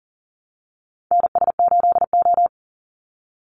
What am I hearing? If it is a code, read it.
Morse code, heard as a keyed tone, about 40 wpm, 700 Hz. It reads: NS8O
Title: Call Signs - 40wpm